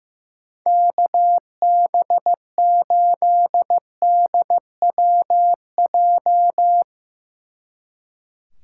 Morse code, heard as a keyed tone, about 15 wpm, 700 Hz. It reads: KB8DWJ
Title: Speed Racing - USA 2x3 Call Signs 15wpm